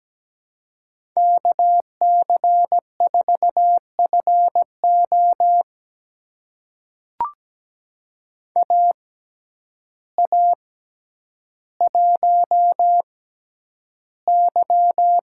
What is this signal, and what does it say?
Morse code, heard as a keyed tone, about 17 wpm, spaced as for 9 wpm, 700 Hz. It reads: KC4FO AA1Y
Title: USA Call Signs - 6x Character Spacing - 17wpm